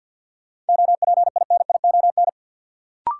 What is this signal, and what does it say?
Morse code, heard as a keyed tone, about 52 wpm, 700 Hz. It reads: OPINION
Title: Sentences from Top 1200 Words - New Word Drill - 50wpm